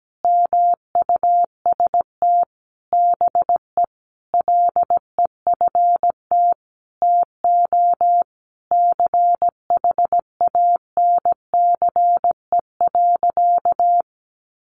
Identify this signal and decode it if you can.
Morse code, heard as a keyed tone, about 17 wpm, 700 Hz. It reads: MUST BE LEFT TO CHANCE.
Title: Sentences from Top 1100 Words 17wpm